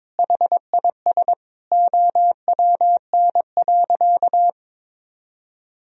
Morse code, heard as a keyed tone, about 22 wpm, 700 Hz. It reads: HIS OWN.
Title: Sentences from Top 1000 Words - Review Part 2 - 22wpm